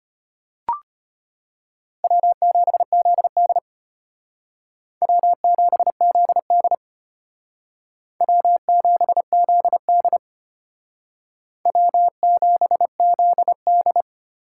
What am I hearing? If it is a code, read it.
Morse code, heard as a keyed tone, about 32 wpm, 700 Hz. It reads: W7ZD W7ZD W7ZD W7ZD